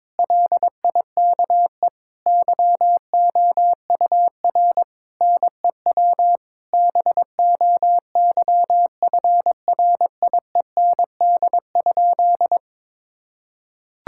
Morse code, heard as a keyed tone, about 22 wpm, 700 Hz. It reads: LIKE YOUR NEW BOYFRIEND?
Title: Sentences from Top 1000 Words - Review Part 5 - 22wpm